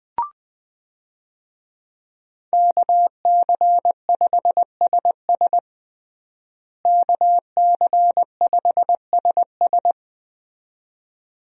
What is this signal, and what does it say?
Morse code, heard as a keyed tone, about 20 wpm, 700 Hz. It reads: KC5SS KC5SS
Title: USA Call Signs - 5 Characters - Repeated 2x - 20wpm